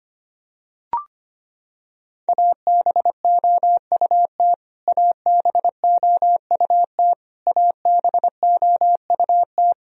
Morse code, heard as a keyed tone, about 25 wpm, 700 Hz. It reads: ABOUT ABOUT ABOUT
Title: Top 100 Words Repeated 25wpm